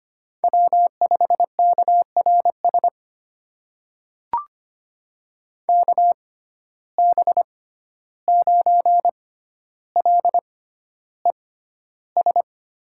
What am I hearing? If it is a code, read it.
Morse code, heard as a keyed tone, about 25 wpm, spaced as for 12 wpm, 700 Hz. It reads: W5KRS KB9LES
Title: USA Call Signs - 6x Character Spacing - 25wpm